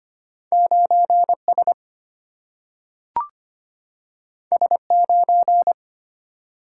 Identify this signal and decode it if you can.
Morse code, heard as a keyed tone, about 25 wpm, 700 Hz. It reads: 9 S S 9
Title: Single-Letter Single-Number 25wpm